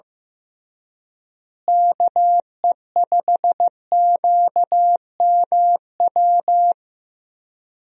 Morse code, heard as a keyed tone, about 15 wpm, 700 Hz. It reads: KE5QMW
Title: USA Call Signs - 6 Characters - Repeated 3x - 15wpm